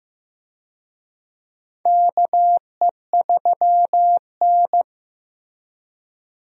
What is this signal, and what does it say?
Morse code, heard as a keyed tone, about 16 wpm, 700 Hz. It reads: KE3N
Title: USA Call Signs - 4 Characters - 15wpm